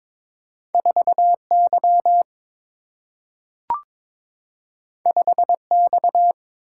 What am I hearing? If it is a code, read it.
Morse code, heard as a keyed tone, about 22 wpm, 700 Hz. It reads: 4 Y 5 X
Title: Single-Letter Single-Number 22wpm